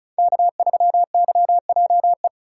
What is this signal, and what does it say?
Morse code, heard as a keyed tone, about 36 wpm, 700 Hz. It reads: K3YJE
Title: USA Call Signs - 5 Characters - 4x Character Spacing - 35wpm